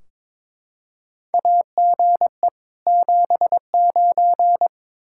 Morse code, heard as a keyed tone, about 22 wpm, 700 Hz. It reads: AGE 79
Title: Practice QSO - Exchange 3 - More Info - 22wpm